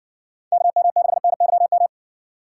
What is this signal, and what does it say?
Morse code, heard as a keyed tone, about 60 wpm, 700 Hz. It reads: KM6NYM